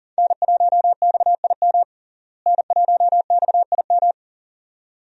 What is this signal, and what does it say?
Morse code, heard as a keyed tone, about 40 wpm, 700 Hz. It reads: N1XIM N1XIM
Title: USA Call Signs - 5 Characters - Repeated 2x - 40wpm